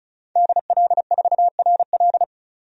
Morse code, heard as a keyed tone, about 34 wpm, 700 Hz. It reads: DL4RL
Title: DXCC-100 Call Signs - 5 Characters - 35wpm